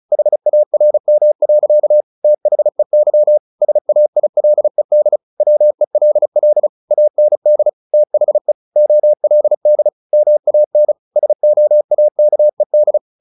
Morse code, heard as a keyed tone, about 34 wpm, 600 Hz. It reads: HARM. THEY SAILED WELL AND THE OLD MAN SOAKED
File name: the old man and the sea 2 35wpm